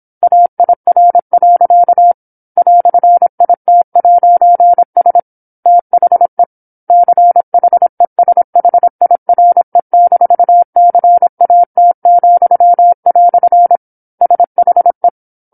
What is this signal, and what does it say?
Morse code, heard as a keyed tone, about 27 wpm, 700 Hz. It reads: AIR. "IT'S THE CHESHIRE-CAT," SHE
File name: 08-Chapter-08 26 WPM